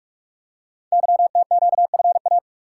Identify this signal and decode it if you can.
Morse code, heard as a keyed tone, about 44 wpm, 700 Hz. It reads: YTQFA
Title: Code Groups - 5 Letters - 45wpm